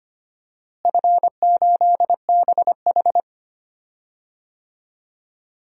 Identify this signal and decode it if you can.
Morse code, heard as a keyed tone, about 25 wpm, 700 Hz. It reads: F8BH
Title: DXCC-100 Call Signs - 4 Characters - Repeated 3x - 25wpm